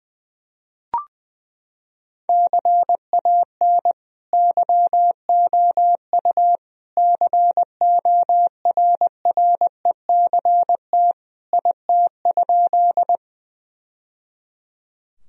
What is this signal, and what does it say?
Morse code, heard as a keyed tone, about 21 wpm, 700 Hz. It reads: CAN YOU CORRECT IT?
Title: Sentences from Top 1000 Words - Review Part 2 - 20wpm